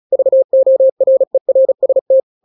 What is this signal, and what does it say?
Morse code, heard as a keyed tone, about 36 wpm, 530 Hz. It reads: VORERST